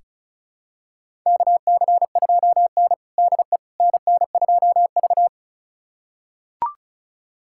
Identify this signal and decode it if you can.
Morse code, heard as a keyed tone, about 36 wpm, 700 Hz. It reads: KC2N DE NN2V
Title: USA Callsign DE Callsign - 4 Characters - 35wpm